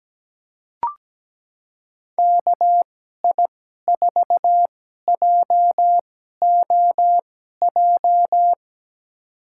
Morse code, heard as a keyed tone, about 17 wpm, spaced as for 12 wpm, 700 Hz. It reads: KI4JOJ
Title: USA Call Signs - 6 Characters - 2x Character Spacing - 17wpm